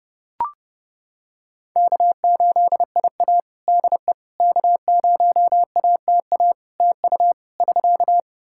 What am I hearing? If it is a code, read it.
Morse code, heard as a keyed tone, about 28 wpm, 700 Hz. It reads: K8IA DE K0ATA TU <SK>
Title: Practice QSO - Exchange 4 - Ending - 30wpm